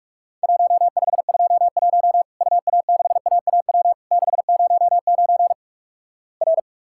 Morse code, heard as a keyed tone, about 46 wpm, 700 Hz. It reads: 1521 UA6AAW 609 R
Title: Worked All Europe DX Contest - QTC Exchange - 45wpm